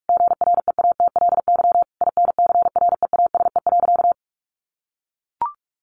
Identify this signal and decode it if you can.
Morse code, heard as a keyed tone, about 44 wpm, 700 Hz. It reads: GREATLY INCREASE.
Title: Sentences from Top 1400 Words 45wpm